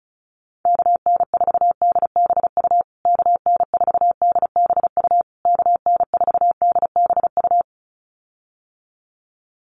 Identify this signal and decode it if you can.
Morse code, heard as a keyed tone, about 34 wpm, 700 Hz. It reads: KN4DBU KN4DBU KN4DBU
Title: USA Call Signs Repeated 3x 35wpm